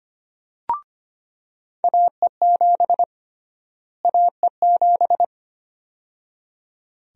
Morse code, heard as a keyed tone, about 25 wpm, 700 Hz. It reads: AE7 AE7
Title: USA Call Sign Prefixes - Repeated 2x - 25wpm